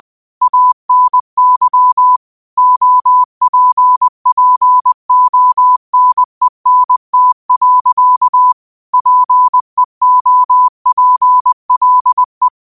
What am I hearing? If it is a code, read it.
Morse code, heard as a keyed tone, about 20 wpm, 1000 Hz. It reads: ANY OPPONENT. PEOPLE
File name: qotdfiles / 20wpm